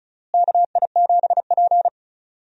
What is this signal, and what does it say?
Morse code, heard as a keyed tone, about 36 wpm, 700 Hz. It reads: KI7P
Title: USA Call Signs - 4 Characters - Repeated 2x - 35wpm